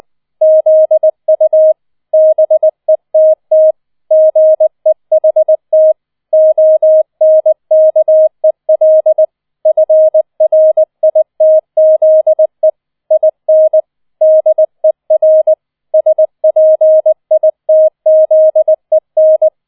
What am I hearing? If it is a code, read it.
Morse code, heard as a keyed tone, about 20 wpm, 630 Hz. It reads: ZU BETT GEHT ONKEL FRITZE IN DER SPITZEN